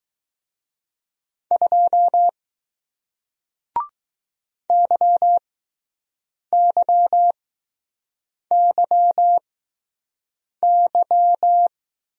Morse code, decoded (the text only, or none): 2 YYYY